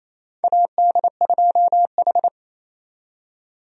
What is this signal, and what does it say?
Morse code, heard as a keyed tone, about 28 wpm, 700 Hz. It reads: AD2H
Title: USA Call Signs - 4 Characters - 28wpm